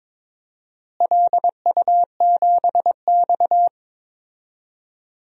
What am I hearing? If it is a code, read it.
Morse code, heard as a keyed tone, about 22 wpm, 700 Hz. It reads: LU7X
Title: DXCC-100 Call Signs - 4 Characters - Repeated 3x - 22wpm